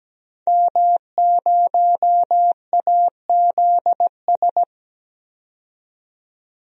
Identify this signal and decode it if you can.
Morse code, heard as a keyed tone, about 17 wpm, 700 Hz. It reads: M0AZS
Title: DXCC-100 Call Signs - 5 Characters - 17wpm